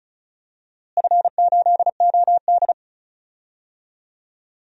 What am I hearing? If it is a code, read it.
Morse code, heard as a keyed tone, about 36 wpm, 700 Hz. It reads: F8OD
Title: DXCC-100 Call Signs - 4 Characters - Repeated 3x - 35wpm